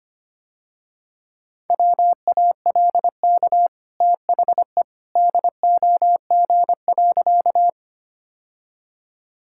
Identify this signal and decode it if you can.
Morse code, heard as a keyed tone, about 25 wpm, 700 Hz. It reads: WALK THE DOG.
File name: Easy Sentences with Hints - 25wpm